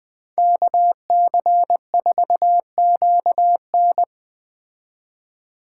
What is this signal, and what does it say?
Morse code, heard as a keyed tone, about 20 wpm, 700 Hz. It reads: KC4QN
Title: Speed Racing - USA Call Signs 17wpm